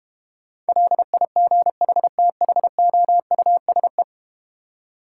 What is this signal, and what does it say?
Morse code, heard as a keyed tone, about 31 wpm, 700 Hz. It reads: LIGHTHOUSE
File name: practice / 25wpm / Difficult Words - Speed-Racing with Hints - 25wpm